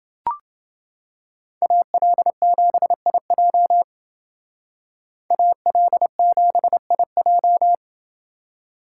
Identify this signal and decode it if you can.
Morse code, heard as a keyed tone, about 28 wpm, 700 Hz. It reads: AL7IJ AL7IJ